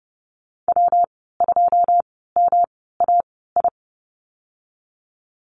Morse code, heard as a keyed tone, about 30 wpm, 700 Hz. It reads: W 2 M A I
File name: Lesson 19g - Letter w - Call Signs 30wpm